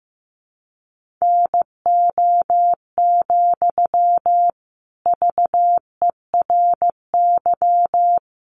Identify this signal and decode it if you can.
Morse code, heard as a keyed tone, about 15 wpm, 700 Hz. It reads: NO, VERY
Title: Sentences from Top 1500 Words 15wpm